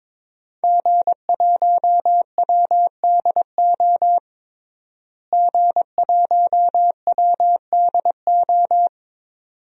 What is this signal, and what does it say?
Morse code, heard as a keyed tone, about 22 wpm, 700 Hz. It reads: G1WDO G1WDO
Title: DXCC-100 Call Signs - 5 Characters - Repeated 2x - 22wpm